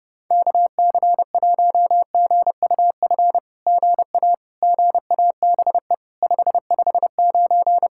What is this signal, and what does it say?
Morse code, heard as a keyed tone, about 30 wpm, 700 Hz. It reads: KC1GUF GA GABE 559
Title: SOTA QSO Practice 30wpm